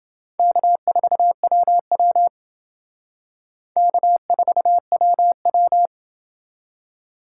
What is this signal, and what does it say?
Morse code, heard as a keyed tone, about 28 wpm, 700 Hz. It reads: K4WW K4WW